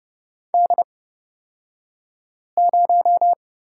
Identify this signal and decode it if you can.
Morse code, heard as a keyed tone, about 30 wpm, spaced as for 5 wpm, 700 Hz. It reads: D0
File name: Lesson 37d - Letter x - Two Characters 30wpm